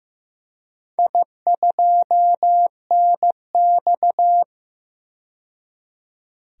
Morse code, heard as a keyed tone, about 15 wpm, 700 Hz. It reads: I2NX
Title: DXCC-100 Call Signs - 4 Characters - 15wpm